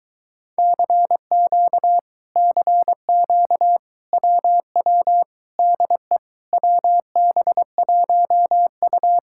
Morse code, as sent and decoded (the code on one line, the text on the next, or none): -.-. --.- / -.-. --.- / .-- .-- / -.. / .-- -... .---- ..-
CQ CQ WW DE WB1U